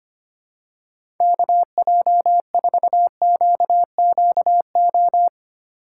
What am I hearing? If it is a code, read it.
Morse code, heard as a keyed tone, about 25 wpm, 700 Hz. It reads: KJ4QQO
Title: USA Call Signs - 6 Characters - 2x Character Spacing - 25wpm